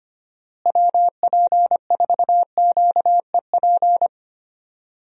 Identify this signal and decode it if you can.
Morse code, heard as a keyed tone, about 25 wpm, 700 Hz. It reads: WP4QEP
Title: USA Call Signs - 6 Characters - Repeated 3x - 25wpm